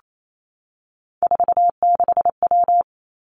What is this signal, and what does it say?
Morse code, heard as a keyed tone, about 28 wpm, 700 Hz. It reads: 46W
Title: Three Challenging Letter-Number Combinations 28wpm